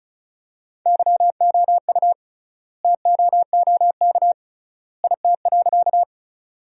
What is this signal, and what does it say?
Morse code, heard as a keyed tone, about 36 wpm, 700 Hz. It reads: YOU TOOK IT.
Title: 3x Word-Spacing - Sentences from Top 300 Words - 35wpm